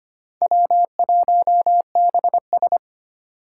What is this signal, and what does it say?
Morse code, heard as a keyed tone, about 25 wpm, 700 Hz. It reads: W1BS
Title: Four Challenging Letter-Number Combinations 25wpm